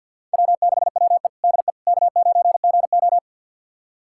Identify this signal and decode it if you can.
Morse code, heard as a keyed tone, about 51 wpm, 700 Hz. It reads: W6WE DE K9GO
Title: USA Callsign DE Callsign - 4 Characters - 50wpm